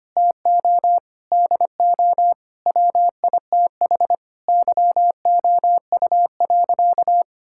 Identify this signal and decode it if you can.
Morse code, heard as a keyed tone, about 25 wpm, 700 Hz. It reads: TO DO WITH YOU.
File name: practice / 25wpm / Sentences with 4-Word Phrases - 25wpm